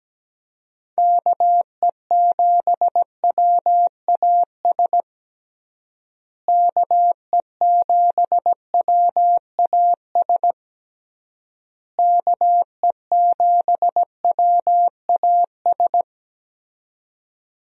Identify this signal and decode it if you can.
Morse code, heard as a keyed tone, about 17 wpm, 700 Hz. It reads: KE7WAS KE7WAS KE7WAS
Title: USA Call Signs - 6 Characters - Repeated 3x - 17wpm